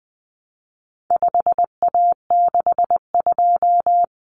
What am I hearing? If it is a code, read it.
Morse code, heard as a keyed tone, about 20 wpm, 700 Hz. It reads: 5A62
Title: Common Cut Numbers - 4-Digit 20wpm